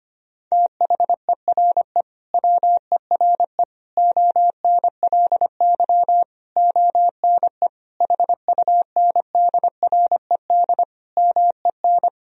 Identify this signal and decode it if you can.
Morse code, heard as a keyed tone, about 25 wpm, 700 Hz. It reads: THERE WERE ONLY ONE HUNDRED MEN
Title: Sentences from Top 1000 Words - Review Part 2 - 25wpm